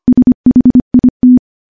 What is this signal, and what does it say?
Morse code, heard as a keyed tone, about 25 wpm, 250 Hz. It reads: SHIT
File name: morse-training-25WPM